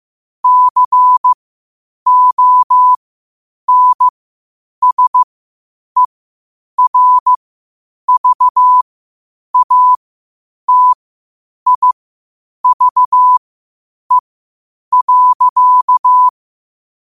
Morse code, heard as a keyed tone, about 15 wpm, spaced as for 8 wpm, 1000 Hz. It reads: CONSERVATIVE.
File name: qotdfiles / 8wpm